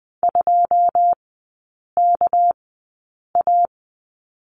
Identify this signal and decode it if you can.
Morse code, heard as a keyed tone, about 20 wpm, spaced as for 8 wpm, 690 Hz. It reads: 2KA
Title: 212 Lesson 34e - Number 8 - Three Characters 20wpm